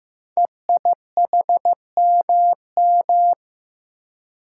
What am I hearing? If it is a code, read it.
Morse code, heard as a keyed tone, about 15 wpm, 700 Hz. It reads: EIHMM
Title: Code Groups - 5 Letters - 15wpm